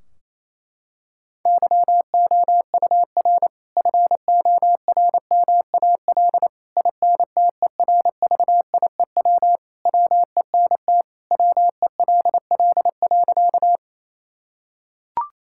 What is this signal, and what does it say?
Morse code, heard as a keyed tone, about 28 wpm, 700 Hz. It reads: YOUR FORMAL INTERVIEW WENT WELL.
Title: Sentences from Top 1600 Words 28wpm